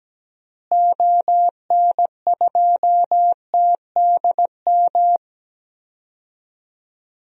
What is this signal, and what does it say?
Morse code, heard as a keyed tone, about 17 wpm, 700 Hz. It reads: ON2TDM
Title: DXCC-100 Call Signs - 6 Characters - Repeated 2x - 17wpm